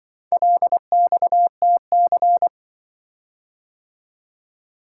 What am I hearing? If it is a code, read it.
Morse code, heard as a keyed tone, about 25 wpm, 700 Hz. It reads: LXTC
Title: Speed-Racing - Four Letter-Number Combinations - 20wpm